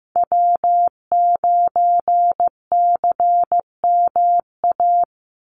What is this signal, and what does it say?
Morse code, heard as a keyed tone, about 15 wpm, 700 Hz. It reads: W9CMA
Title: USA Call Signs 15wpm